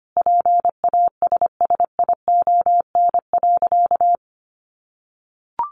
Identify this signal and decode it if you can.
Morse code, heard as a keyed tone, about 25 wpm, 700 Hz. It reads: PASSION.
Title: Sentences from Top 1300 Words 25wpm